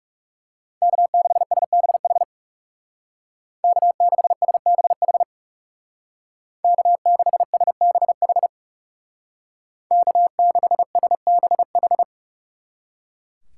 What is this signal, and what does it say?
Morse code, heard as a keyed tone, about 39 wpm, 700 Hz. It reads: K6SBH K6SBH K6SBH K6SBH